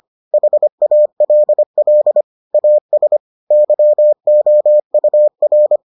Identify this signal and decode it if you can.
Morse code, heard as a keyed tone, about 25 wpm, 600 Hz. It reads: HALL AS YOUR